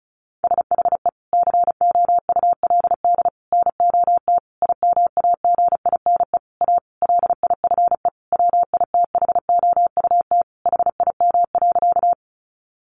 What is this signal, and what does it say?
Morse code, heard as a keyed tone, about 34 wpm, 700 Hz. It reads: SHE COULD NOT IMAGINE A LIFE WITHOUT HIM.
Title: Sentences from Top 1400 Words 35wpm